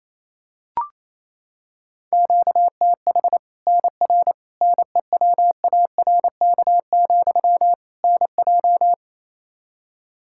Morse code, heard as a keyed tone, about 28 wpm, 700 Hz. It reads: QTH NR NEWARK, NJ
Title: Practice QSO - Exchange 1 - Beginning - 28wpm